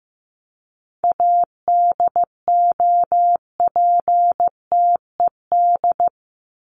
Morse code, heard as a keyed tone, about 15 wpm, 700 Hz. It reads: ADOPTED
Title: ED Suffix - Top 100 Words with Base Words 15wpm